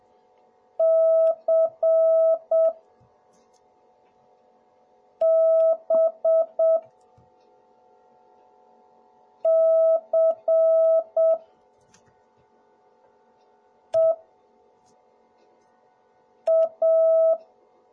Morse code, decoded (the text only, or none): CBCEA